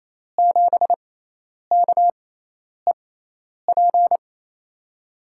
Morse code, heard as a keyed tone, about 28 wpm, spaced as for 10 wpm, 700 Hz. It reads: 7KEP
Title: USA Call Signs - 6x Character Spacing - 28wpm